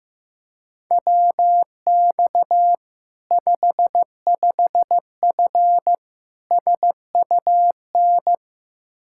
Morse code, heard as a keyed tone, about 15 wpm, 700 Hz. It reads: WX 55F SUN
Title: Practice QSO - Exchange 2 - Info - 15wpm